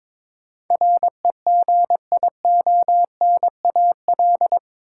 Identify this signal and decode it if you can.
Morse code, heard as a keyed tone, about 22 wpm, 700 Hz. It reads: REGIONAL
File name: practice / 22wpm / Eight Letter Words - 22wpm